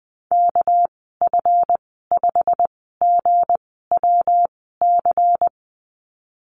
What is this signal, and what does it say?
Morse code, heard as a keyed tone, about 20 wpm, spaced as for 14 wpm, 700 Hz. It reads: KF5GWC
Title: 200 Lesson 32g - Letter k - Call Signs 20wpm